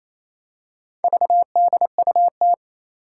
Morse code, heard as a keyed tone, about 28 wpm, 700 Hz. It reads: VDUT
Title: Four Challenging Letter-Number Combinations 28wpm